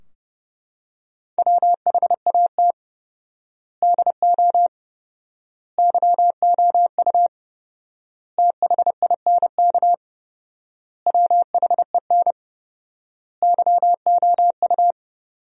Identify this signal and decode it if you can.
Morse code, heard as a keyed tone, about 30 wpm, 700 Hz. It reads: WHAT DO YOU THINK WHEN YOU
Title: 4x Word-Spacing - Sentences from Top 300 Words - 30wpm